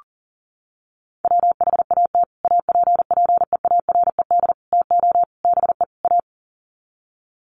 Morse code, heard as a keyed tone, about 40 wpm, 700 Hz. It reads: WHAT APPEARED TO BE A
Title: Sets of 5 Words Encore2 40wpm